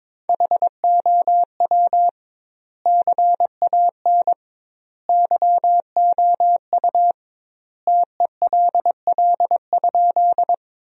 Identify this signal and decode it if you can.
Morse code, heard as a keyed tone, about 22 wpm, 700 Hz. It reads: HOW CAN YOU TELL?
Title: 2x Word-Spacing - Sentences from Top 200 Words - 22wpm